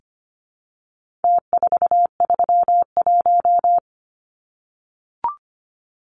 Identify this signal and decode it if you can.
Morse code, heard as a keyed tone, about 25 wpm, 700 Hz. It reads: T431
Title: Numbers - Contest Style - 4-Digit 25wpm